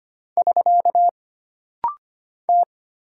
Morse code, heard as a keyed tone, about 25 wpm, 700 Hz. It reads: <SK> T
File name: Single Character (Letter:Number:Punctuation:Prosign) - Rapid-Fire - 25wpm